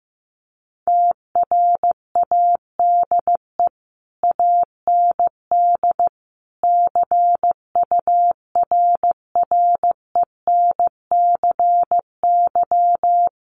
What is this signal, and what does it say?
Morse code, heard as a keyed tone, about 14 wpm, 700 Hz. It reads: TRADE AND CURRENCY
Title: Sentences from Top 1500 Words 15wpm